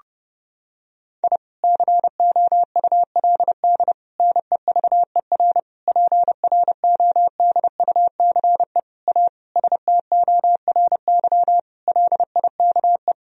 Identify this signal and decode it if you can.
Morse code, heard as a keyed tone, about 32 wpm, 700 Hz. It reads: I COULD NEVER PRODUCE A STORY LIKE
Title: Sentences from Top 1000 Words - Review Part 3 - 30wpm